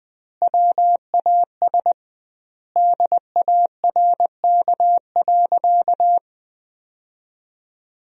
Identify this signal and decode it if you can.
Morse code, heard as a keyed tone, about 21 wpm, 700 Hz. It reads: WAS DARK.
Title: 2x Word-Spacing - Sentences from Top 500 Words - 20wpm